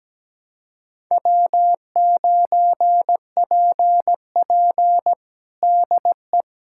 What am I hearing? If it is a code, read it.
Morse code, heard as a keyed tone, about 17 wpm, 700 Hz. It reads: W9PP DE
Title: USA Callsign DE Callsign - 4 Characters - 17wpm